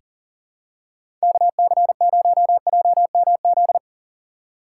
Morse code, heard as a keyed tone, about 37 wpm, 700 Hz. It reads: KC0JMZ
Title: USA Call Signs - 6 Characters - 4x Character Spacing - 40wpm